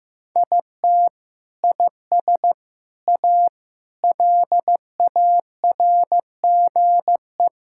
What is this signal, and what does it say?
Morse code, heard as a keyed tone, about 15 wpm, 700 Hz. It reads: IT IS A LARGE
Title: Sentences from Top 1000 Words - Review Part 5 - 15wpm